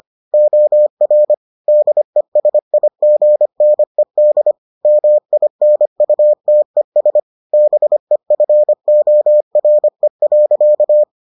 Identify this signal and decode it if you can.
Morse code, heard as a keyed tone, about 25 wpm, 600 Hz. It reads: OR DESIGNED MINUTES BEFORE.